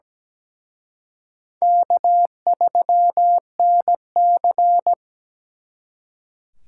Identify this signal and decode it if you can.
Morse code, heard as a keyed tone, about 20 wpm, 700 Hz. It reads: K3NC
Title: Speed Racing - USA 1x2 Call Signs 17wpm